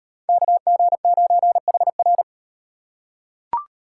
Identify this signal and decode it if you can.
Morse code, heard as a keyed tone, about 38 wpm, 700 Hz. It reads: KG9HR